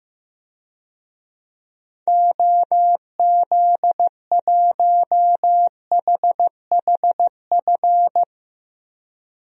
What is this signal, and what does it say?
Morse code, heard as a keyed tone, about 15 wpm, 700 Hz. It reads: OZ1HHF
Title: DXCC-100 Call Signs - 6 Characters - 15wpm